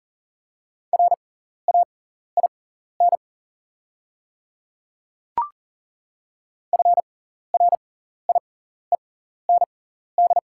Morse code, heard as a keyed tone, about 40 wpm, spaced as for 14 wpm, 690 Hz. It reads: RAIN FRIEND
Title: 6x Character Spacing - Top 300 Words - 40wpm